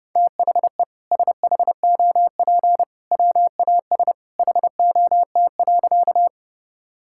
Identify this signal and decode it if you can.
Morse code, heard as a keyed tone, about 30 wpm, 700 Hz. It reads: THE SHOP WAS HOT.
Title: Sentences from Top 1600 Words 30wpm